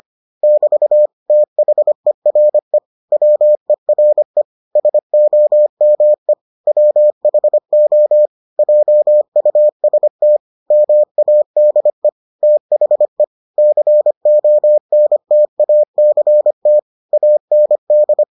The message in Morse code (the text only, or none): <BT> THERE WERE SOME WHO JUST MADE THE CONTACT AND